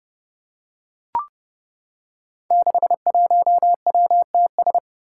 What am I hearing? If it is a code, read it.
Morse code, heard as a keyed tone, about 30 wpm, 700 Hz. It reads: 61WTS
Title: Five Challenging Letter-Number Combinations 30wpm